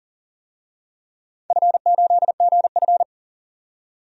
Morse code, heard as a keyed tone, about 41 wpm, 700 Hz. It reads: F8GF
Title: DXCC-100 Call Signs - 4 Characters - 40wpm